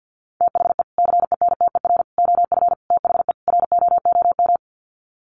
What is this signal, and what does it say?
Morse code, heard as a keyed tone, about 48 wpm, 700 Hz. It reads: THE CENTER OF THE ROOM
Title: Sets of 5 Words Encore 50wpm